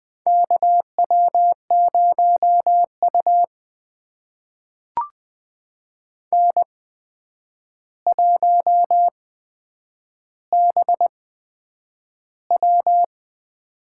Morse code, heard as a keyed tone, about 20 wpm, spaced as for 9 wpm, 700 Hz. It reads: KW0U N1BW
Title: USA Call Signs - 4 Characters - 8x Character Spacing - 20wpm